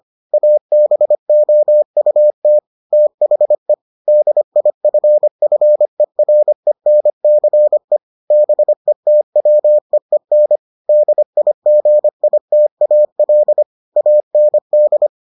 Morse code, heard as a keyed tone, about 25 wpm, 600 Hz. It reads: ABOUT THE DIFFERENCE BETWEEN DIGITAL AND